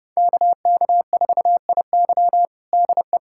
Morse code, heard as a keyed tone, about 30 wpm, 700 Hz. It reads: KK4IY DE
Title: USA Callsign DE Callsign - 5 Characters - 30wpm